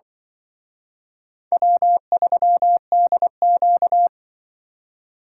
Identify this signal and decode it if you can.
Morse code, heard as a keyed tone, about 23 wpm, 700 Hz. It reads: W3DQ
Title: Speed Racing - USA 1x2 Call Signs 20wpm